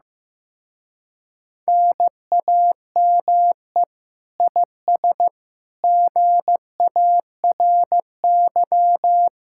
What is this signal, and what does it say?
Morse code, heard as a keyed tone, about 15 wpm, 700 Hz. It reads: NAME IS GARY
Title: Speed-Racing Common QSO Phrases Encore - 15wpm